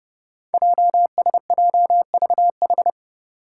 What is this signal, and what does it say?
Morse code, heard as a keyed tone, about 30 wpm, 700 Hz. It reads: JSJVH
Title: Five Challenging Letter-Number Combinations 30wpm